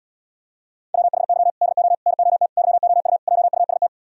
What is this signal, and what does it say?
Morse code, heard as a keyed tone, about 77 wpm, 690 Hz. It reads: WHY DO NOT YOU JUST